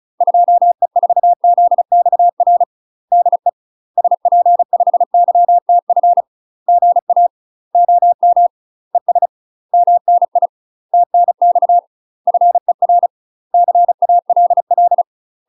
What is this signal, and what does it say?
Morse code, heard as a keyed tone, about 34 wpm, 700 Hz. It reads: 2E4ZXR DE SP5YTF GA OM ES MNI TNX FER CALL
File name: UK MASTER QSOs TRAINING / 35WPM / UK MASTER QSOs 0005